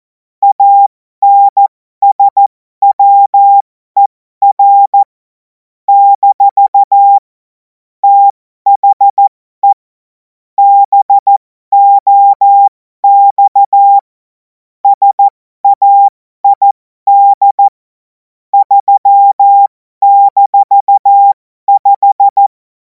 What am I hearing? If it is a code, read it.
Morse code, heard as a keyed tone, about 14 wpm, spaced as for 12 wpm, 800 Hz. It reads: ANSWER - THE BOX SAID 3-5
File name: Morsekurs / Witze